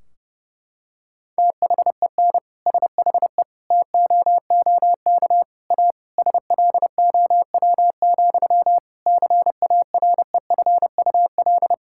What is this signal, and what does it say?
Morse code, heard as a keyed tone, about 30 wpm, 700 Hz. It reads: THEN SHE TOOK A SLOW, CAREFUL
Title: Sentences from Top 1600 Words 30wpm